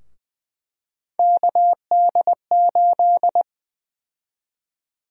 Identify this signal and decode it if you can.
Morse code, heard as a keyed tone, about 20 wpm, 700 Hz. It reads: KD8
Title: USA Call Sign Prefixes - 20wpm